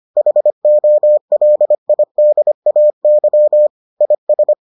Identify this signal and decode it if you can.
Morse code, heard as a keyed tone, about 25 wpm, 600 Hz. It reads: HOLIDAY IS